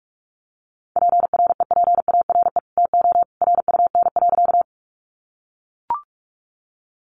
Morse code, heard as a keyed tone, about 44 wpm, 700 Hz. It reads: PREPARE TO RUN.
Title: Sentences from Top 900 Words 45wpm